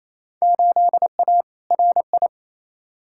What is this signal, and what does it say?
Morse code, heard as a keyed tone, about 28 wpm, 700 Hz. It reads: 8A RI
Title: Field Day Exchanges Encore - 28wpm